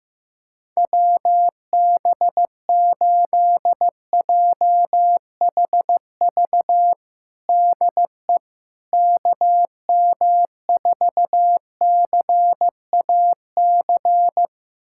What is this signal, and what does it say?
Morse code, heard as a keyed tone, about 15 wpm, 700 Hz. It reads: WB8JHV DE KM4CAC
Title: USA Callsign DE Callsign 15wpm